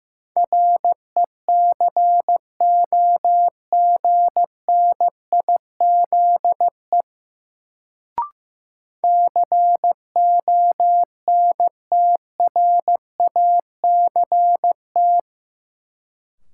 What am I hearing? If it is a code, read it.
Morse code, heard as a keyed tone, about 15 wpm, 700 Hz. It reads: RECOGNIZE CONTRACT
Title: Sentences from Top 1400 Words - New Word Drill - 15wpm